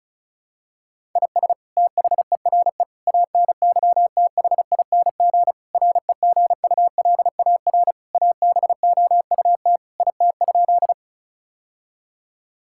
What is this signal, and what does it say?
Morse code, heard as a keyed tone, about 36 wpm, 700 Hz. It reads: IS THERE ANYTHING REGULAR ABOUT IT?
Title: Sentences from Top 1000 Words - Review Part 6 - 35wpm